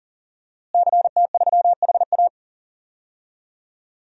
Morse code, heard as a keyed tone, about 40 wpm, 700 Hz. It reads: CT3HA
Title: DXCC-100 Call Signs - 5 Characters - Repeated 2x - 40wpm